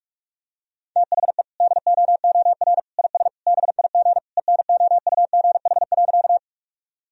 Morse code, heard as a keyed tone, about 46 wpm, 700 Hz. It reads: THE DOOR IS BIG ENOUGH.
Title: Sentences from Top 1000 Words - Review Part 1 - 45wpm